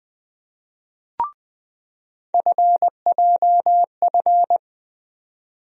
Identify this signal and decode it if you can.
Morse code, heard as a keyed tone, about 21 wpm, 700 Hz. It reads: FJF
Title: USA Call Sign Postfixes - 3 Letters 20wpm